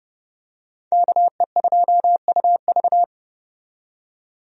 Morse code, heard as a keyed tone, about 29 wpm, 700 Hz. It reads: KE2UV
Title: USA Call Signs - 5 Characters - 30wpm